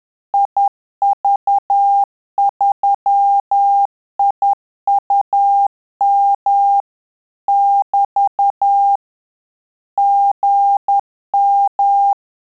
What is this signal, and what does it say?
Morse code, heard as a keyed tone, about 10 wpm, 800 Hz. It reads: IV3IUM <BT> GM
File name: Telegraphy-group / QSO CW 10 PPM